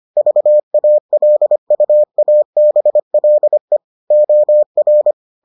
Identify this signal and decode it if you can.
Morse code, heard as a keyed tone, about 25 wpm, 600 Hz. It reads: VALUABLE OR